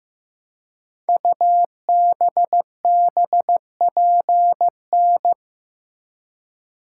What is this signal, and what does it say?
Morse code, heard as a keyed tone, about 15 wpm, 700 Hz. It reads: UBBPN
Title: Code Groups - 5 Letter:Numbers - 15wpm